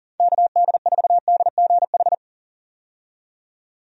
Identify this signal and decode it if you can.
Morse code, heard as a keyed tone, about 40 wpm, 690 Hz. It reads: KD4DGH
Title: USA Call Signs - 6 Characters - 40wpm